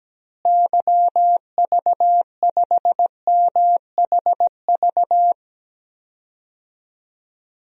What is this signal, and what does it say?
Morse code, heard as a keyed tone, about 17 wpm, 700 Hz. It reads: YV5MHV
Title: DXCC-100 Call Signs - 6 Characters - Repeated 3x - 17wpm